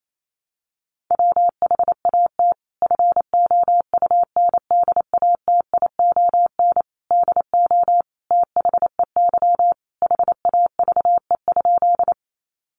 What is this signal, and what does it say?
Morse code, heard as a keyed tone, about 28 wpm, 700 Hz. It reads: WHAT FOUNDATION DO THEY HAVE?
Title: Sentences from Top 1200 Words 28wpm